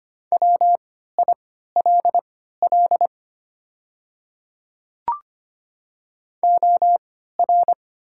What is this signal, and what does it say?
Morse code, heard as a keyed tone, about 25 wpm, spaced as for 14 wpm, 700 Hz. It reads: WILL OR
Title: 3x Character Spacing - Top 100 Words - 25wpm